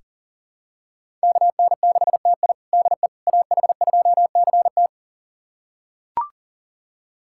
Morse code, heard as a keyed tone, about 40 wpm, 700 Hz. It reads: KN6TI DE AH2CT
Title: USA Callsign DE Callsign - 5 Characters - 40wpm